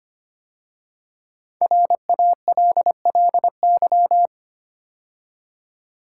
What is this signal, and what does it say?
Morse code, heard as a keyed tone, about 25 wpm, 700 Hz. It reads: RALLY
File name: 5 Letter Words - First 3 Letters Given - 25wpm